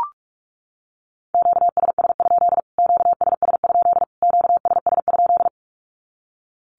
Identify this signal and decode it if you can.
Morse code, heard as a keyed tone, about 44 wpm, 700 Hz. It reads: QSS? QSS? QSS?
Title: Q-Codes Repeated 3x 45wpm